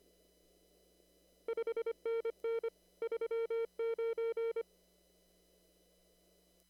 Morse code, heard as a keyed tone, about 23 wpm, 470 Hz. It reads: 5NN 39
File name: CW / kt5j wpx cw-op-k5pi